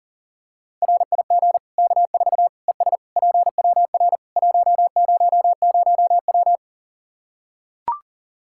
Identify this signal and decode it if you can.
Morse code, heard as a keyed tone, about 40 wpm, 700 Hz. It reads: RIG K4 ES PWR 100W
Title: Practice QSO - Exchange 2 - Info - 40wpm